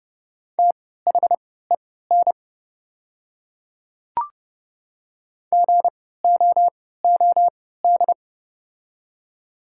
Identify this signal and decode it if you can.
Morse code, heard as a keyed tone, about 30 wpm, spaced as for 17 wpm, 700 Hz. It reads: THEN GOOD
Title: 3x Character Spacing - Top 200 Words - 30wpm